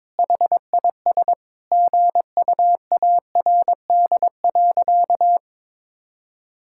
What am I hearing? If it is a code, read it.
Morse code, heard as a keyed tone, about 21 wpm, 700 Hz. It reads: HIS GUARD.
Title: Sentences from Top 1000 Words - Review Part 5 - 22wpm